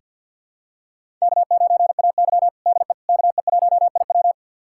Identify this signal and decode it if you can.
Morse code, heard as a keyed tone, about 47 wpm, 700 Hz. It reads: K9AY DE KE1IW
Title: USA Callsign DE Callsign 50wpm